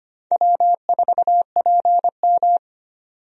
Morse code, heard as a keyed tone, about 25 wpm, 700 Hz. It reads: W4PM
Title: USA Call Signs - 4 Characters - 25wpm